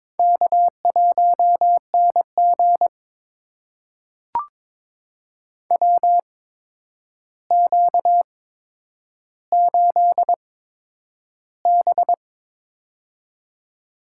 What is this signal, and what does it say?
Morse code, heard as a keyed tone, about 22 wpm, spaced as for 9 wpm, 700 Hz. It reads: K1NG WQ8B
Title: USA Call Signs - 4 Characters - 8x Character Spacing - 22wpm